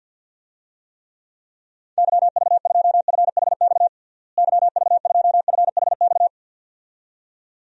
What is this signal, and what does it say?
Morse code, heard as a keyed tone, about 49 wpm, 700 Hz. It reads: YV2FHX YV2FHX
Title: DXCC-100 Call Signs - 6 Characters - Repeated 2x - 50wpm